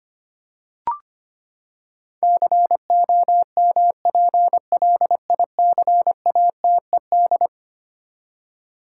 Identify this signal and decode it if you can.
Morse code, heard as a keyed tone, about 25 wpm, 700 Hz. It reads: COMPLICATED
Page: morse-code-ninja-practice-25wpm